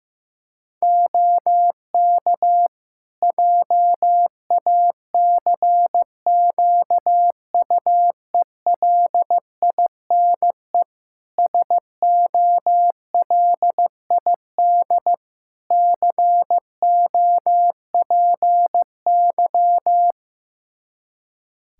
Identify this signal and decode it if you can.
Morse code, heard as a keyed tone, about 15 wpm, 700 Hz. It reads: OK JACQUELINE SOLID COPY
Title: K1USN SST QSO Practice 15wpm